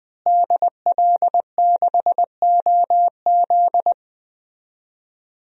Morse code, heard as a keyed tone, about 20 wpm, 700 Hz. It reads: DL6OZ
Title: DXCC-100 Call Signs - 5 Characters - Repeated 2x - 20wpm